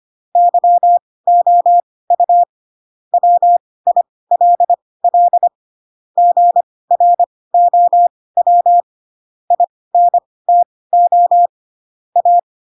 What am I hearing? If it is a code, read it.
Morse code, heard as a keyed tone, about 25 wpm, spaced as for 18 wpm, 700 Hz. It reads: YOU WILL GROW INTO A
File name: Pinocchio / Pinocchio ch4 12 25 ik1hge